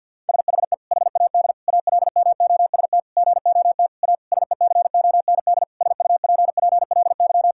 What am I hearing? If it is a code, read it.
Morse code, heard as a keyed tone, about 52 wpm, 700 Hz. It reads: SHE HAD ALMOST GOT A SECOND SUPPLY
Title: Sentences from Top 1000 Words - Review Part 5 - 50wpm